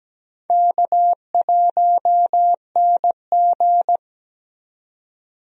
Morse code, heard as a keyed tone, about 17 wpm, 700 Hz. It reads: K1NG
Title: USA Call Signs - 4 Characters - 17wpm